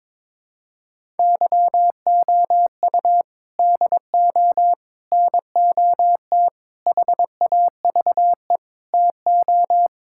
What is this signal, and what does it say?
Morse code, heard as a keyed tone, about 22 wpm, 700 Hz. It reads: YOU DO NOT HAVE TO
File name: practice / 22wpm / Sentences with 4-Word Phrases - 22wpm